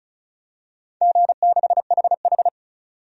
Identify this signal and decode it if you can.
Morse code, heard as a keyed tone, about 34 wpm, 700 Hz. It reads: G6HH
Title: DXCC-100 Call Signs - 4 Characters - 35wpm